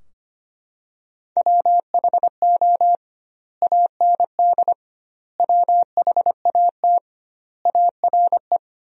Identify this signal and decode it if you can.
Morse code, heard as a keyed tone, about 25 wpm, 700 Hz. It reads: WHO AND WHAT ARE
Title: 2x Word-Spacing - Sentences from Top 100 Words - 25wpm